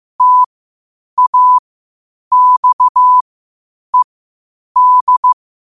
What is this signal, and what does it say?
Morse code, heard as a keyed tone, about 15 wpm, spaced as for 8 wpm, 1000 Hz. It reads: TAXED
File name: qotdfiles / 8wpm